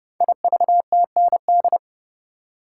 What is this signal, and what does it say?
Morse code, heard as a keyed tone, about 28 wpm, 700 Hz. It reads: IVTND
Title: Code Groups - 5 Letters - 30wpm